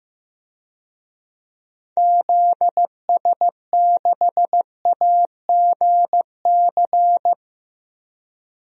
Morse code, heard as a keyed tone, about 15 wpm, 700 Hz. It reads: ZS6AGC
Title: DXCC-100 Call Signs - 6 Characters - 15wpm